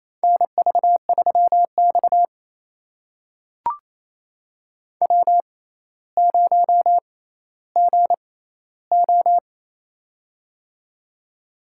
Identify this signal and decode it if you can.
Morse code, heard as a keyed tone, about 28 wpm, spaced as for 14 wpm, 700 Hz. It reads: NV3X W0GO
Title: USA Call Signs - 4 Characters - 6x Character Spacing - 28wpm